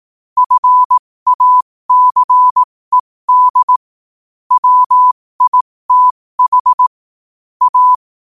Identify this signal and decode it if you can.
Morse code, heard as a keyed tone, about 18 wpm, spaced as for 15 wpm, 1000 Hz. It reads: FACED WITH A
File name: qotdfiles / 15wpm